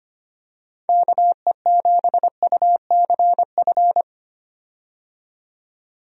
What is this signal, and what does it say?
Morse code, heard as a keyed tone, about 25 wpm, 700 Hz. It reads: KE7UCF
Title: USA Call Signs - 6 Characters - 25wpm